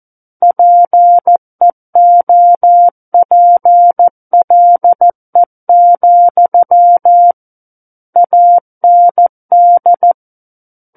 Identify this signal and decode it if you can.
Morse code, heard as a keyed tone, about 14 wpm, 700 Hz. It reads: PEOPLE, AND
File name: Book One / 05-Chapter 5 - 14 WPM